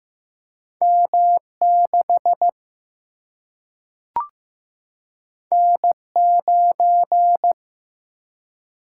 Morse code, heard as a keyed tone, about 15 wpm, 700 Hz. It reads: M6 N9
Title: Single-Letter Single-Number 15wpm